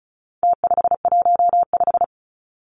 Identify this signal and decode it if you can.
Morse code, heard as a keyed tone, about 34 wpm, 700 Hz. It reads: T515
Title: Numbers - Contest Style - 4-Digit 35wpm